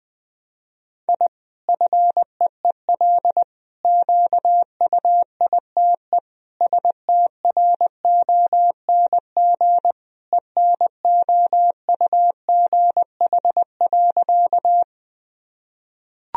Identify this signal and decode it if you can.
Morse code, heard as a keyed tone, about 20 wpm, 700 Hz. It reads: I FEEL QUITE STRONG ENOUGH.
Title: Sentences from Top 1000 Words - Review Part 3 - 20wpm